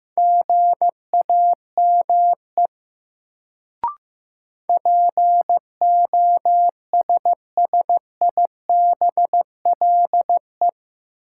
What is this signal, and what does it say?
Morse code, heard as a keyed tone, about 15 wpm, 700 Hz. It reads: GAME POSSIBLE
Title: Sentences from Top 500 Words - New Word Drill - 15wpm